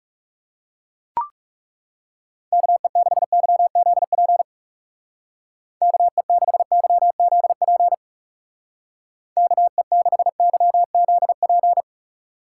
Morse code, heard as a keyed tone, about 36 wpm, 700 Hz. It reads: KE6YZP KE6YZP KE6YZP
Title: Speed Racing - USA 2x3 Call Signs 30wpm